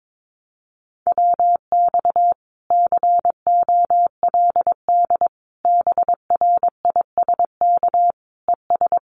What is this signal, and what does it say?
Morse code, heard as a keyed tone, about 22 wpm, 700 Hz. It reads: WX COLD BRISK ES
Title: Common QSO Phrases Encore 22wpm